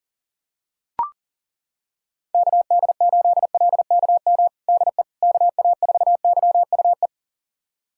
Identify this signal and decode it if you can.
Morse code, heard as a keyed tone, about 40 wpm, 700 Hz. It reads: KD8LKM DE KA4YUE
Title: USA Callsign DE Callsign - 6 Characters - 40wpm